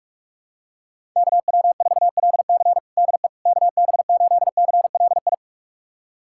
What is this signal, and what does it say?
Morse code, heard as a keyed tone, about 46 wpm, 700 Hz. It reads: KW4LC DE KB8CLI
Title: USA Callsign DE Callsign 45wpm